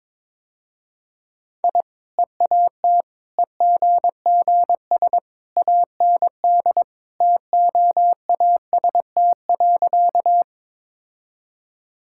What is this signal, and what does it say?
Morse code, heard as a keyed tone, about 22 wpm, 700 Hz. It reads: I EAT EGGS AND TOAST.
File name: practice / 22wpm / Easy Sentences with Hints - 22wpm